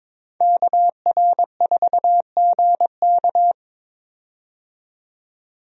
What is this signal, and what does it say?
Morse code, heard as a keyed tone, about 22 wpm, 700 Hz. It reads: KR4GK
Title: USA Call Signs - 5 Characters - 22wpm